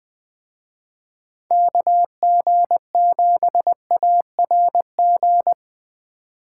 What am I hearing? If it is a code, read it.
Morse code, heard as a keyed tone, about 23 wpm, 700 Hz. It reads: KG7ARG
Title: Speed Racing - USA 2x3 Call Signs 20wpm